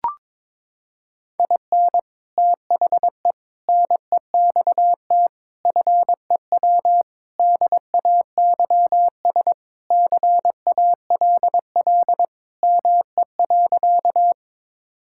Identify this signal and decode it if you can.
Morse code, heard as a keyed tone, about 21 wpm, 700 Hz. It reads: IN THE NEXT FEW DAYS CALL ME.
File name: practice / 22wpm / Sentences with 3-Word Phrases - 22wpm